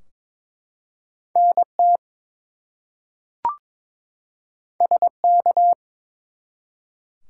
Morse code, heard as a keyed tone, about 23 wpm, 700 Hz. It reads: NT SK
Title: US States and Canadian Provinces 22wpm